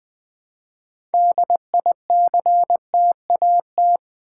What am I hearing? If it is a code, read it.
Morse code, heard as a keyed tone, about 20 wpm, 700 Hz. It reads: DICTAT
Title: Build A Word - 2 Letters At A Time - 8-Letter Words - 20wpm